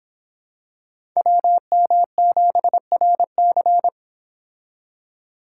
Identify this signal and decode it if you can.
Morse code, heard as a keyed tone, about 26 wpm, 700 Hz. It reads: WM7RC